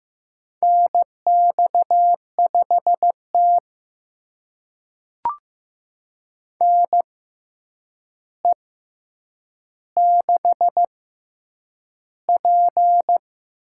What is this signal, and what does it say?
Morse code, heard as a keyed tone, about 15 wpm, spaced as for 8 wpm, 700 Hz. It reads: NX5T NE6P
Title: USA Call Signs - 4 Characters - 6x Character Spacing - 15wpm